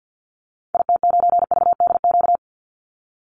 Morse code, heard as a keyed tone, about 50 wpm, 700 Hz. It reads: IT9VDQ
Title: Call Signs - 50wpm